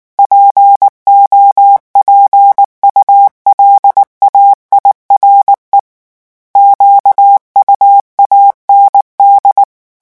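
Morse code, heard as a keyed tone, about 19 wpm, 800 Hz. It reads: POPULAIRE QUAND
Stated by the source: French Morse revision 8 drill of random five-letter groups at 18 words per minute.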